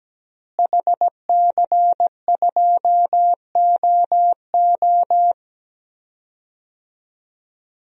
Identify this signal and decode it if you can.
Morse code, heard as a keyed tone, about 17 wpm, 700 Hz. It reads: HC2OO
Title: DXCC-100 Call Signs - 5 Characters - Repeated 2x - 17wpm